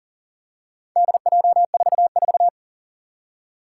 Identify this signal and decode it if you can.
Morse code, heard as a keyed tone, about 40 wpm, 700 Hz. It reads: DJ44
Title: Four Challenging Letter-Number Combinations 40wpm